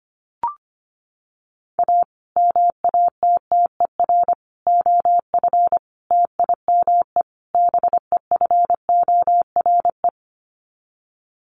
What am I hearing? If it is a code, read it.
Morse code, heard as a keyed tone, about 25 wpm, 700 Hz. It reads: A MATTER OF TIME BEFORE
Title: Sets of 5 Words Encore 25wpm